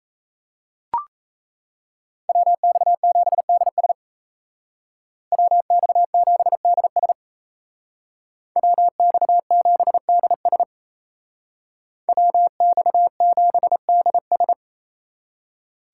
Morse code, heard as a keyed tone, about 37 wpm, 700 Hz. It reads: WX7DS WX7DS WX7DS WX7DS